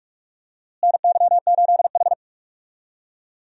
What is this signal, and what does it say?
Morse code, heard as a keyed tone, about 46 wpm, 700 Hz. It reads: NY8H
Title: USA Call Signs - 4 Characters - Repeated 3x - 45wpm